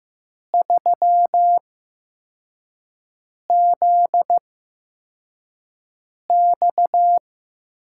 Morse code, heard as a keyed tone, about 15 wpm, spaced as for 4 wpm, 700 Hz. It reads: 3ZX
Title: USA Call Signs - 5 Characters - 8x Character Spacing - 15wpm